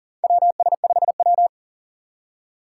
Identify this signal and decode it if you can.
Morse code, heard as a keyed tone, about 40 wpm, 700 Hz. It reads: WS5W
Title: Four Challenging Letter-Number Combinations 40wpm